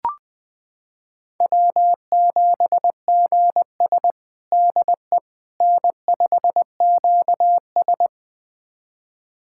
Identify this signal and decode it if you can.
Morse code, heard as a keyed tone, about 20 wpm, 700 Hz. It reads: W7GS DE N5QS
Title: USA Callsign DE Callsign - 4 Characters - 20wpm